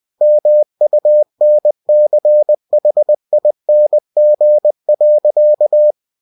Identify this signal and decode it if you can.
Morse code, heard as a keyed tone, about 20 wpm, 600 Hz. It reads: MUNCHING.
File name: cw-books / winnie-the-pooh / winnie-20wpm-page-0084